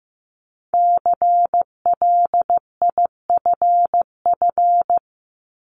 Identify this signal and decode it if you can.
Morse code, heard as a keyed tone, about 15 wpm, 700 Hz. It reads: CLIFF
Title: Multi-letter Phonemes Repeated 3x 15wpm